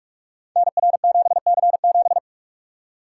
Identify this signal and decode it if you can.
Morse code, heard as a keyed tone, about 44 wpm, 700 Hz. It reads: NR7C7
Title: Code Groups - 5 Letter:Numbers - 45wpm